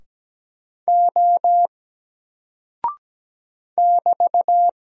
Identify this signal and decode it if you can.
Morse code, heard as a keyed tone, about 17 wpm, 700 Hz. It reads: O <BT>
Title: Single Character (Letter:Number:Punctuation:Prosign) Alt 17wpm